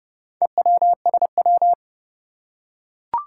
Morse code, heard as a keyed tone, about 30 wpm, 700 Hz. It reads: EWSW
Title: Four Challenging Letter-Number Combinations 30wpm